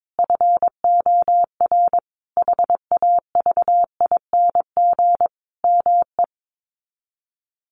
Morse code, heard as a keyed tone, about 22 wpm, 700 Hz. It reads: FOR HAVING ME
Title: Sets of 5 Words Encore 22wpm